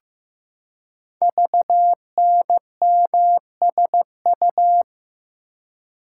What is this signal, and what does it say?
Morse code, heard as a keyed tone, about 15 wpm, 700 Hz. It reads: VNMSU
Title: Code Groups - 5 Letters - 15wpm